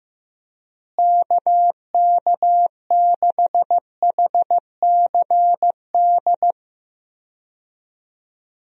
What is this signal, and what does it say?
Morse code, heard as a keyed tone, about 15 wpm, 700 Hz. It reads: KK6HCD
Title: USA Call Signs - 6 Characters - Repeated 3x - 15wpm